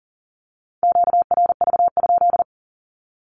Morse code, heard as a keyed tone, about 42 wpm, 700 Hz. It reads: QRV?
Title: Q-Codes with Spoken Meaning - 40wpm